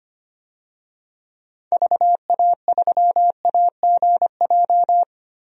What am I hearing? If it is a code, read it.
Morse code, heard as a keyed tone, about 25 wpm, 700 Hz. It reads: VA3AGJ
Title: DXCC-100 Call Signs - 6 Characters - Repeated 3x - 25wpm